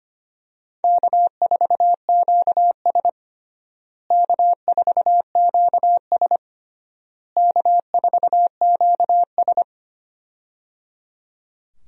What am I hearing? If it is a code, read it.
Morse code, heard as a keyed tone, about 25 wpm, 700 Hz. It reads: K4QS K4QS K4QS
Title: USA Call Signs - 4 Characters - Repeated 3x - 25wpm